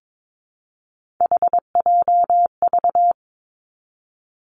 Morse code, heard as a keyed tone, about 22 wpm, 700 Hz. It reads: HJV
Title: Three Challenging Letter-Number Combinations 22wpm